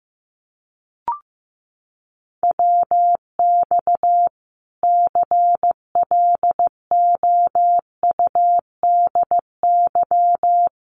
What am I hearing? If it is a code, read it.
Morse code, heard as a keyed tone, about 15 wpm, 700 Hz. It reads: WX CLOUDY
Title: Common QSO Phrases 15wpm